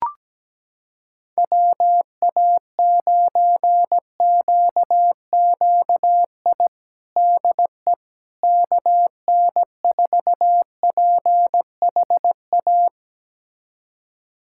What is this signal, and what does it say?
Morse code, heard as a keyed tone, about 17 wpm, 700 Hz. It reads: WA9QQI DE KN4PHA
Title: USA Callsign DE Callsign 17wpm